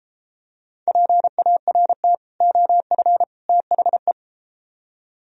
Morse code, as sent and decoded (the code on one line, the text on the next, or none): .--. .- .-. - / --- ..-. / - .... .
PART OF THE